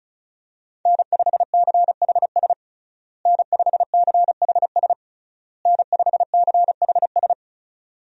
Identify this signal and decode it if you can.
Morse code, heard as a keyed tone, about 34 wpm, 700 Hz. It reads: N5CHS N5CHS N5CHS
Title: USA Call Signs - 5 Characters - Repeated 3x - 35wpm